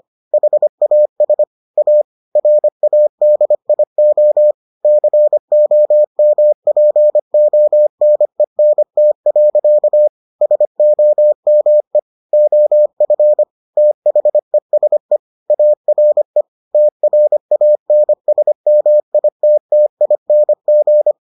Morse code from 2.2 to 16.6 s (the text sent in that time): RADIO COMPONENT. SOME OF THESE ARE